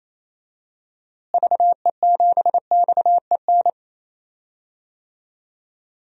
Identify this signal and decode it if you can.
Morse code, heard as a keyed tone, about 28 wpm, 700 Hz. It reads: VE7XEN
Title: DXCC-100 Call Signs - 6 Characters - Repeated 2x - 28wpm